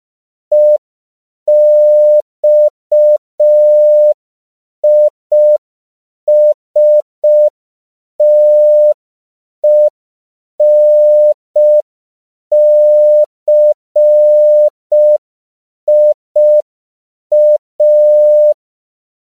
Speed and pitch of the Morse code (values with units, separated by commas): 5 wpm, 600 Hz